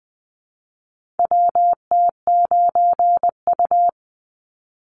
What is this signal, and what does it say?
Morse code, heard as a keyed tone, about 20 wpm, 700 Hz. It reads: WT9U
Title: Call Signs Repeated 3x - 20wpm